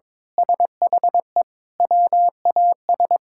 Morse code, heard as a keyed tone, about 22 wpm, 700 Hz. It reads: SHE WAS
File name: practice / 22wpm / Sentences with 4-Word Phrases - 22wpm